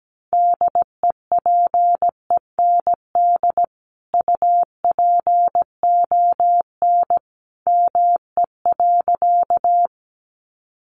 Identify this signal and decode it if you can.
Morse code, heard as a keyed tone, about 17 wpm, 700 Hz. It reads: DEPEND UPON ME.
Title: Sentences from Top 1300 Words 17wpm